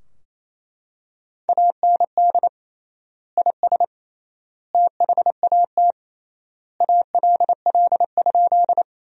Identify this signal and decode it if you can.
Morse code, heard as a keyed tone, about 28 wpm, 700 Hz. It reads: AND IS THAT ALL?
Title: 3x Word-Spacing - Sentences from Top 100 Words - 28wpm